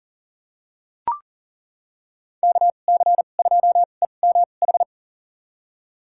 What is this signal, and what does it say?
Morse code, heard as a keyed tone, about 40 wpm, spaced as for 29 wpm, 700 Hz. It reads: KC2EMH
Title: USA Call Signs - 6 Characters - 2x Character Spacing - 40wpm